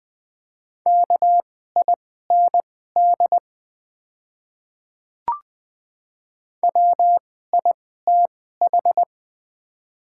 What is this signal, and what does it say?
Morse code, heard as a keyed tone, about 21 wpm, spaced as for 15 wpm, 700 Hz. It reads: KIND WITH
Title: 2x Character Spacing - Top 200 Words - 20wpm